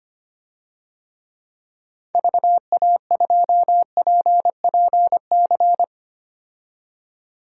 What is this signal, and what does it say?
Morse code, heard as a keyed tone, about 25 wpm, 700 Hz. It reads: VA2PPC
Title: DXCC-100 Call Signs - 6 Characters - 25wpm